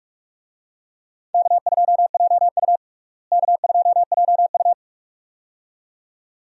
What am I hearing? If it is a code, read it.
Morse code, heard as a keyed tone, about 43 wpm, 700 Hz. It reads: K2JU K2JU
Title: USA Call Signs - 4 Characters - Repeated 2x - 45wpm